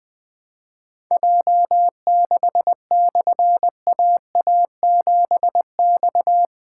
Morse code, heard as a keyed tone, about 20 wpm, 700 Hz. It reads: J6/AA7X
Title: Difficult Call Signs 20wpm